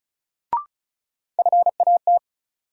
Morse code, heard as a keyed tone, about 34 wpm, 700 Hz. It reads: FAT
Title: Sentences from Top 600 Words - New Word Drill - 35wpm